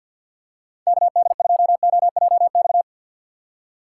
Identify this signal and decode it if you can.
Morse code, heard as a keyed tone, about 46 wpm, 700 Hz. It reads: KD2OJX
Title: USA Call Signs - 4x Character Spacing - 50wpm